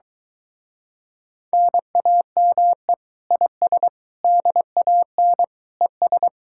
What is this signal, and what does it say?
Morse code, heard as a keyed tone, about 22 wpm, 700 Hz. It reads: NAME IS DAN ES